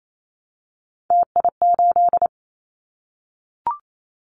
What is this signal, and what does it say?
Morse code, heard as a keyed tone, about 28 wpm, 700 Hz. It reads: TI8
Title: Callsign Prefixes for Top 100 DXCC - 28wpm